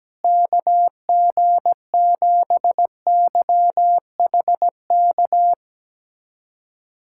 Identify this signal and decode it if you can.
Morse code, heard as a keyed tone, about 17 wpm, 700 Hz. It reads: KG7YHK
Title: USA Callsign DE Callsign 17wpm